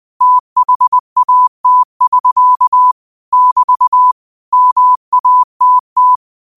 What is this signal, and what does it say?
Morse code, heard as a keyed tone, about 20 wpm, 1000 Hz. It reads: THAT <SK> <BT> MATT